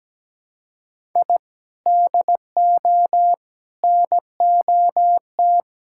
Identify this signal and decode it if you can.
Morse code, heard as a keyed tone, about 17 wpm, 700 Hz. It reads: I DO NOT
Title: Sentences from Top 1000 Words - Review Part 6 - 17wpm